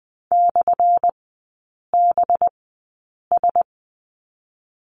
Morse code, heard as a keyed tone, about 20 wpm, spaced as for 8 wpm, 700 Hz. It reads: /BS
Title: 191 Lesson 31e - Letter v - Three Characters 20wpm